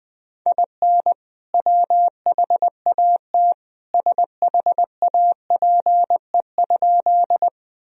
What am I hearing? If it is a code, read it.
Morse code, heard as a keyed tone, about 20 wpm, 700 Hz. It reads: IN WHAT SHAPE?
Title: Sentences from Top 1000 Words - Review Part 3 - 20wpm